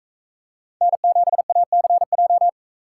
Morse code, heard as a keyed tone, about 42 wpm, 700 Hz. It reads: N7ACJ